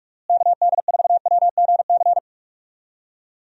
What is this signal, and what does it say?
Morse code, heard as a keyed tone, about 43 wpm, 700 Hz. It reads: KD4WGC
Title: USA Call Signs - 6 Characters - Repeated 3x - 45wpm